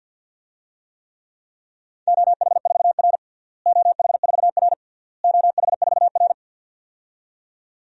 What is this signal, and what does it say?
Morse code, heard as a keyed tone, about 51 wpm, 700 Hz. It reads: OH4R OH4R OH4R
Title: DXCC-100 Call Signs - 4 Characters - Repeated 3x - 50wpm